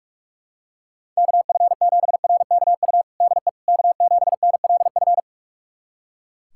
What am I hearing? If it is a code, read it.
Morse code, heard as a keyed tone, about 41 wpm, 700 Hz. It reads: KF7RKU DE K7NLF
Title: USA Callsign DE Callsign 45wpm